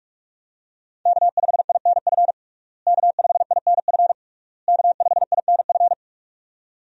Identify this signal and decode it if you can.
Morse code, heard as a keyed tone, about 41 wpm, 700 Hz. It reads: K5INF K5INF K5INF
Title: USA Call Signs - 5 Characters - Repeated 3x - 45wpm